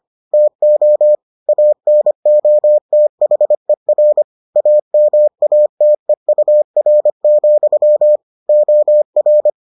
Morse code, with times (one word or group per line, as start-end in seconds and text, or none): 0.3-1.1 TO
1.5-4.2 ANOTHER
4.6-8.2 AMATEUR,
8.5-9.5 OR